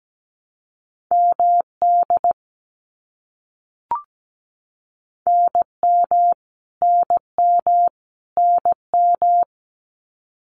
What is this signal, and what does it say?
Morse code, heard as a keyed tone, about 17 wpm, 700 Hz. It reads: MD NM NM NM
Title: US States Repeated 3x 17wpm